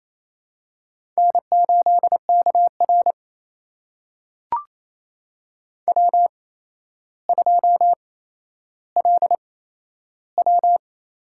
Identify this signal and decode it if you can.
Morse code, heard as a keyed tone, about 27 wpm, spaced as for 12 wpm, 700 Hz. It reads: N8KR W2LW
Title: USA Call Signs - 4 Characters - 8x Character Spacing - 28wpm